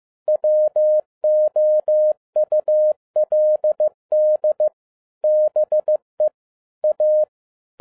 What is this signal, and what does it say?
Morse code, heard as a keyed tone, about 15 wpm, 610 Hz. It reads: WOULD BE A
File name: 2011-07-mp3 / n47 JOHN K3TN 15wpm